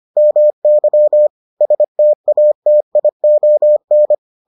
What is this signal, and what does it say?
Morse code, heard as a keyed tone, about 25 wpm, 600 Hz. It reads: MY STATION